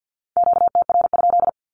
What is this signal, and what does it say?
Morse code, heard as a keyed tone, about 48 wpm, 700 Hz. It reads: QTR?
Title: Q-Codes 50wpm